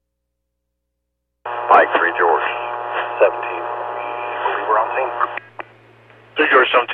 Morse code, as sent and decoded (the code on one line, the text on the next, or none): none